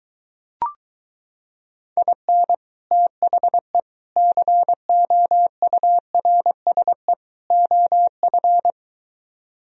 Morse code, heard as a keyed tone, about 23 wpm, 700 Hz. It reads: IN THE COURSE OF